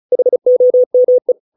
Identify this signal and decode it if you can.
Morse code, heard as a keyed tone, about 34 wpm, 500 Hz. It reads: HOME